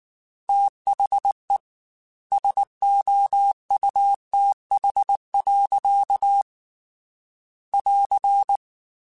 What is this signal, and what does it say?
Morse code, heard as a keyed tone, about 19 wpm, 800 Hz. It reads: THE SOUTH. <AR>
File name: Weather Sets / Weather Sets-mp3s / Weather 4 10 @18